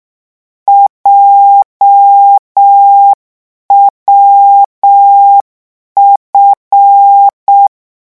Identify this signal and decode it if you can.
Morse code, heard as a keyed tone, about 6 wpm, 800 Hz. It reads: JWF